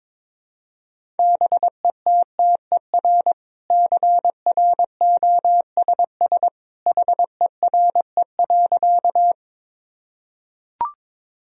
Morse code, heard as a keyed tone, about 21 wpm, 700 Hz. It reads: BETTER CROSS HERE.
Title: Sentences from Top 1000 Words - Review Part 1 - 22wpm